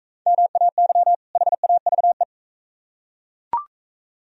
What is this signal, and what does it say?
Morse code, heard as a keyed tone, about 40 wpm, 700 Hz. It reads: MAY HAVE